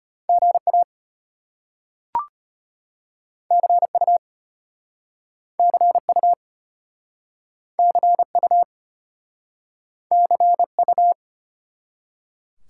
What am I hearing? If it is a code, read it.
Morse code, heard as a keyed tone, about 34 wpm, 700 Hz. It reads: GA CU CU CU CU